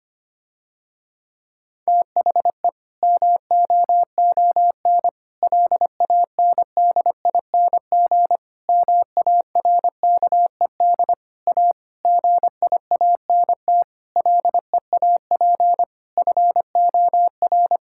Morse code, heard as a keyed tone, about 25 wpm, 700 Hz. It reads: THE MOON LANDING MARKED A GIANT LEAP FOR
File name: practice / 25wpm / Difficult Sentences with Hints - 25wpm